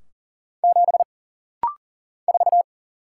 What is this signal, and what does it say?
Morse code, heard as a keyed tone, about 40 wpm, 700 Hz. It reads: 7 4
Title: Single Letter-Number Alt - Rapid-Fire 40wpm